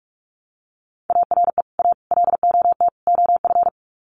Morse code, heard as a keyed tone, about 44 wpm, 700 Hz. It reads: ARE A LOT OF
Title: Sets of 4 Words 45wpm